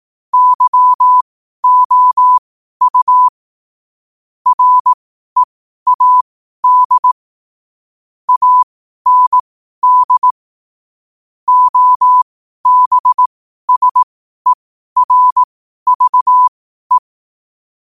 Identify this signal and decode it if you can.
Morse code, heard as a keyed tone, about 18 wpm, spaced as for 12 wpm, 1000 Hz. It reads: YOU READ AND OBSERVE
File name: qotdfiles / 12wpm